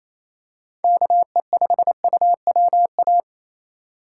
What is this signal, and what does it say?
Morse code, heard as a keyed tone, about 28 wpm, 700 Hz. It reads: KE5UWA
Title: USA Call Signs - 6 Characters - 4x Character Spacing - 28wpm